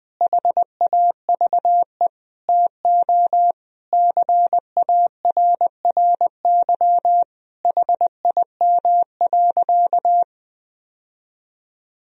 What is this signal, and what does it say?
Morse code, heard as a keyed tone, about 20 wpm, 700 Hz. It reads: HAVE TO CARRY HIM.
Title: Sentences from Top 1000 Words - Review Part 1 - 20wpm